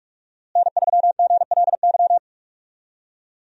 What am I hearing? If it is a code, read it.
Morse code, heard as a keyed tone, about 46 wpm, 700 Hz. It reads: N3GLY
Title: USA Call Signs - 5 Characters - Repeated 3x - 45wpm